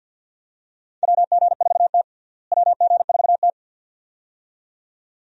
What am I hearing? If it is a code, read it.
Morse code, heard as a keyed tone, about 50 wpm, 700 Hz. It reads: WG4T WG4T
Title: USA Call Signs - 4 Characters - Repeated 2x - 50wpm